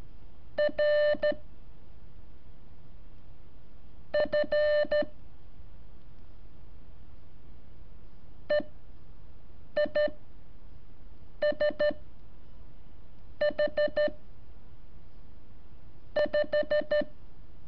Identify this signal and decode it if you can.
Morse code, heard as a keyed tone, about 13 wpm, spaced as for 5 wpm, 600 Hz. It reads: R F EISH 5
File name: curso / ejercicio 21